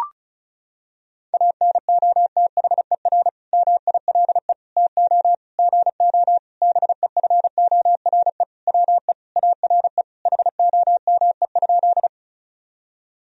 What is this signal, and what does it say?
Morse code, heard as a keyed tone, about 34 wpm, 700 Hz. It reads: ANOTHER MILE TO GO BEFORE WE ARE HOME?
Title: Sentences from Top 1000 Words - Review Part 1 - 35wpm